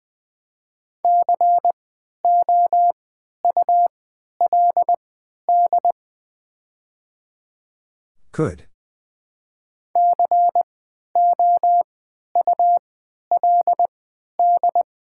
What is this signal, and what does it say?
Morse code, heard as a keyed tone, about 20 wpm, spaced as for 11 wpm, 700 Hz. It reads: COULD COULD
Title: 3x Character Spacing - Top 200 Words - 20wpm